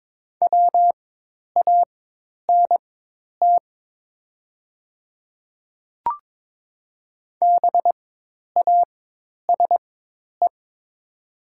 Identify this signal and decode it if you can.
Morse code, heard as a keyed tone, about 22 wpm, spaced as for 10 wpm, 700 Hz. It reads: WANT BASE
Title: 4x Character Spacing - Top 300 Words - 22wpm